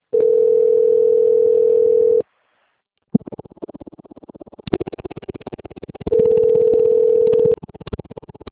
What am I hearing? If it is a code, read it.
Ringback tone (North America), 440 + 480 Hz.
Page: 191.252.59.82 Recife - PE